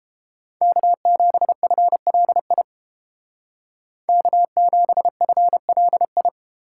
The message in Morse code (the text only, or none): K7FLI K7FLI